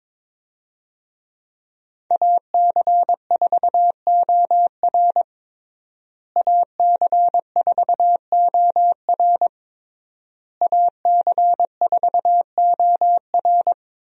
Morse code, heard as a keyed tone, about 22 wpm, 700 Hz. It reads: AC4OR AC4OR AC4OR
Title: USA Call Signs - 5 Characters - Repeated 3x - 22wpm